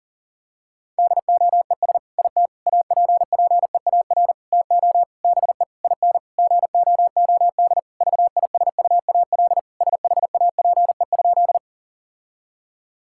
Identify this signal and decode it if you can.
Morse code, heard as a keyed tone, about 40 wpm, 700 Hz. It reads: DOES IT APPEAR TO BE IN GOOD VISUAL SHAPE?
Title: Sentences from Top 1000 Words - Review Part 6 - 40wpm